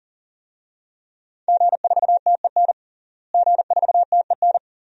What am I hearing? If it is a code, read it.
Morse code, heard as a keyed tone, about 40 wpm, 700 Hz. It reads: G4TEN G4TEN
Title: DXCC-100 Call Signs - 5 Characters - Repeated 2x - 40wpm